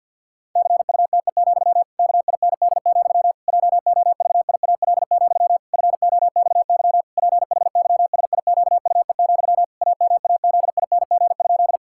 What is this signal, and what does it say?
Morse code, heard as a keyed tone, about 51 wpm, 700 Hz. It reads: CUTE, KIND, JOVIAL, FOXY PHYSIQUE, AMAZING?
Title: Pangram Challenge v2 - 50wpm